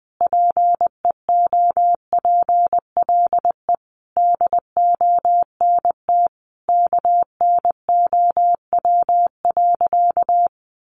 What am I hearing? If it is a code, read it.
Morse code, heard as a keyed tone, about 20 wpm, 700 Hz. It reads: PEOPLE DONT KNOW.
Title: Sentences from Top 600 Words 20wpm